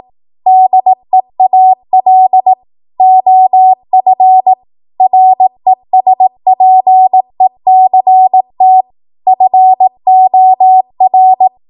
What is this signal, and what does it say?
Morse code, heard as a keyed tone, about 18 wpm, 750 Hz. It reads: DEAL OF RESPECT FOR
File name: Archive / 18 WPM / 18WPM